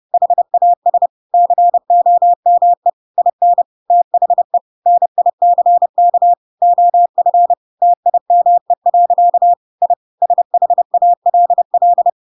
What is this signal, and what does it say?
Morse code, heard as a keyed tone, about 30 wpm, 700 Hz. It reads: HAS COME IN THE NICK OF TIME. I SHALL
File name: Pinocchio / Pinocchio ch1 30 30 ik1hge